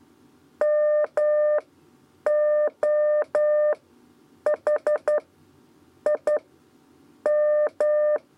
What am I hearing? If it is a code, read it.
Morse code, heard as a keyed tone, about 11 wpm, spaced as for 8 wpm, 600 Hz. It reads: MOH I M
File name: Lezione trasmessa 2 lez 2 exe